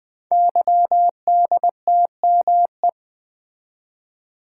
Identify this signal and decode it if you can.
Morse code, heard as a keyed tone, about 21 wpm, 700 Hz. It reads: YDTME
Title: Code Groups - 5 Letters - 20wpm